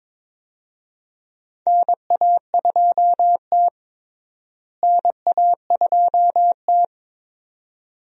Morse code, heard as a keyed tone, about 22 wpm, 700 Hz. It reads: NA2T NA2T
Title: USA Call Signs - 4 Characters - Repeated 2x - 22wpm